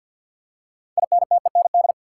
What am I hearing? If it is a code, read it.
Morse code, heard as a keyed tone, about 51 wpm, 700 Hz. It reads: INTEND
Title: Six Letter Words - 50wpm